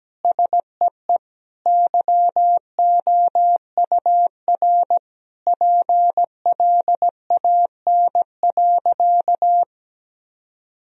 Morse code, heard as a keyed tone, about 17 wpm, 700 Hz. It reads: SEE YOUR PLAN.
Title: Sentences from Top 1000 Words - Review Part 2 - 17wpm